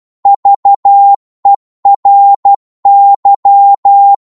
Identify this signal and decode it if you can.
Morse code, heard as a keyed tone, about 12 wpm, 800 Hz. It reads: VERY
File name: cwmp3 / psg-1-12wpm